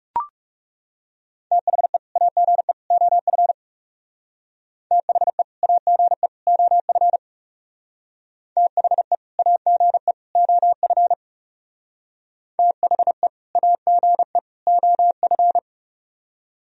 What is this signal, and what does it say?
Morse code, heard as a keyed tone, about 40 wpm, 700 Hz. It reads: THE AGE OF THE AGE OF THE AGE OF THE AGE OF